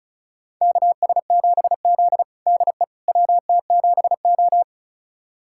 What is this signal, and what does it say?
Morse code, heard as a keyed tone, about 36 wpm, 700 Hz. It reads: KS7Z DE WT7O
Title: USA Callsign DE Callsign - 4 Characters - 35wpm